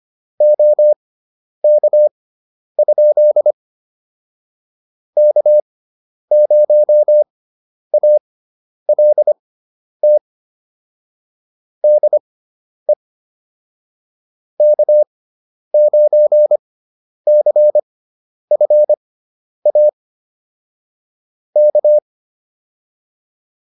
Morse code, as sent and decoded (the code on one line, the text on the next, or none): --- -.- ..--.. / -.- ----- .- .-.. - / -.. / -.- ----. -.-. ..-. .- / -.-
OK? K0ALT DE K9CFA K